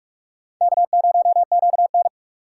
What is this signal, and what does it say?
Morse code, heard as a keyed tone, about 43 wpm, 700 Hz. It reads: K0QN
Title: USA Call Signs - 4 Characters - 4x Character Spacing - 45wpm